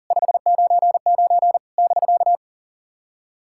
Morse code, heard as a keyed tone, about 40 wpm, 700 Hz. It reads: 599 <BK>
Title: SOTA QSO Practice 40wpm